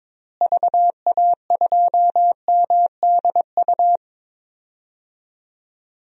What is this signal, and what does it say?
Morse code, heard as a keyed tone, about 22 wpm, 700 Hz. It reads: VA2MDU
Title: DXCC-100 Call Signs - 6 Characters - Repeated 3x - 22wpm